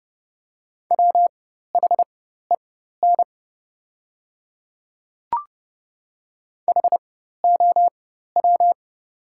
Morse code, heard as a keyed tone, about 30 wpm, spaced as for 14 wpm, 700 Hz. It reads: WHEN HOW
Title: 4x Character Spacing - Top 200 Words - 30wpm